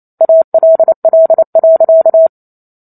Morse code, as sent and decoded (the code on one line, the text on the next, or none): .- .-.. .-.. .-.-.-
ALL.